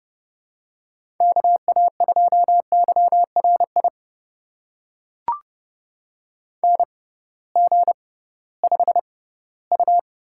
Morse code, heard as a keyed tone, about 31 wpm, spaced as for 18 wpm, 700 Hz. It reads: KA2YRI NG5U
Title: USA Call Signs - 6x Character Spacing - 30wpm